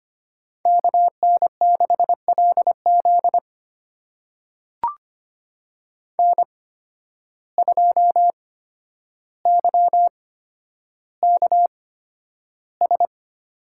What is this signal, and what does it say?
Morse code, heard as a keyed tone, about 25 wpm, spaced as for 11 wpm, 700 Hz. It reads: KN6LZ N2YKS
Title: USA Call Signs - 5 Characters - 8x Character Spacing - 25wpm